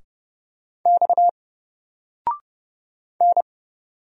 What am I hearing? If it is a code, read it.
Morse code, heard as a keyed tone, about 30 wpm, 700 Hz. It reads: X N
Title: Single Letter-Number 30wpm